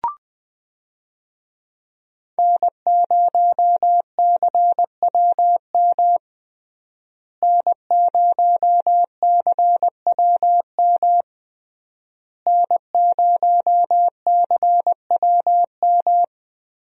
Morse code, heard as a keyed tone, about 20 wpm, 700 Hz. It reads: N0CWM N0CWM N0CWM
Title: USA Call Signs - 5 Characters - Repeated 3x - 20wpm